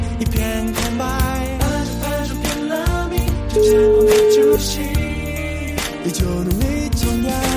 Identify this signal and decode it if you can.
Ringback tone (Europe, 425 Hz), with noise.